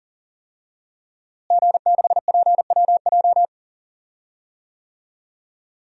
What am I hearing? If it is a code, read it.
Morse code, heard as a keyed tone, about 40 wpm, 700 Hz. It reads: G6PWJ
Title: DXCC-100 Call Signs - 5 Characters - Repeated 3x - 40wpm